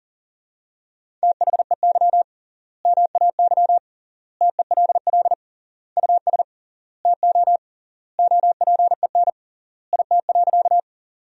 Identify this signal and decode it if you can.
Morse code, heard as a keyed tone, about 37 wpm, 700 Hz. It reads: THEY MAY TELL US TO OPEN IT.
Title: 3x Word-Spacing - Sentences from Top 300 Words - 40wpm